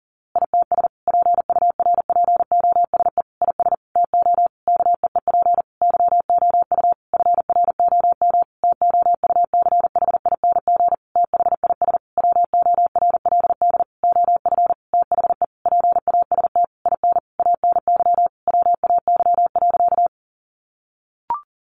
Morse code, heard as a keyed tone, about 40 wpm, 700 Hz. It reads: ITS PURPOSE IS TO KEEP YOU FROM TOUCHING THIS WORLD OF THE PAST IN ANY WAY.
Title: A Sound Of Thunder 40wpm